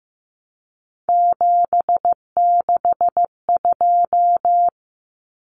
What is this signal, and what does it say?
Morse code, heard as a keyed tone, about 15 wpm, 700 Hz. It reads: 762
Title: Numbers - 3-Digit 15wpm